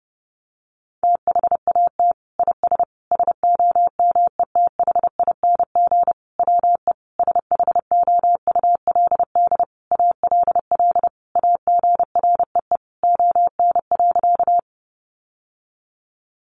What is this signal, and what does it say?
Morse code, heard as a keyed tone, about 30 wpm, 700 Hz. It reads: THAT IS SOMETHING WE SHOULD ALL AGREE ON.
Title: Sentences from Top 1100 Words 30wpm